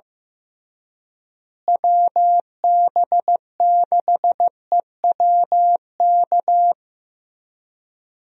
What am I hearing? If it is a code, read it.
Morse code, heard as a keyed tone, about 15 wpm, 700 Hz. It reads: WB6EWK
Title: USA Call Signs - 6 Characters - Repeated 3x - 15wpm